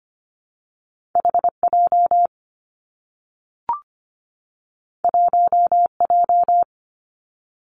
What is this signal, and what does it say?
Morse code, heard as a keyed tone, about 25 wpm, 700 Hz. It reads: H J 1 J
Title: Two Challenging Letter-Number Combinations 25wpm